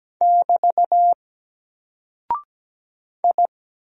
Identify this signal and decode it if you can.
Morse code, heard as a keyed tone, about 17 wpm, 700 Hz. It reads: <BT> I
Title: Single Character (Letter:Number:Punctuation:Prosign) 17wpm